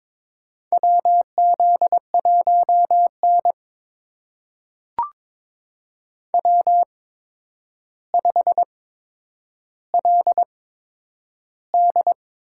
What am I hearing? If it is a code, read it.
Morse code, heard as a keyed tone, about 22 wpm, spaced as for 9 wpm, 700 Hz. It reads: WZ1N W5LD
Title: USA Call Signs - 4 Characters - 8x Character Spacing - 22wpm